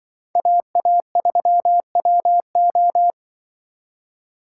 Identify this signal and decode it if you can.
Morse code, heard as a keyed tone, about 23 wpm, 700 Hz. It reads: AA3WO
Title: Speed Racing - USA 2x2 Call Signs 20wpm